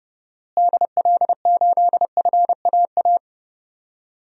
Morse code, heard as a keyed tone, about 30 wpm, 700 Hz. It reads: DL8FAA
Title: DXCC-100 Call Signs - 6 Characters - Repeated 2x - 30wpm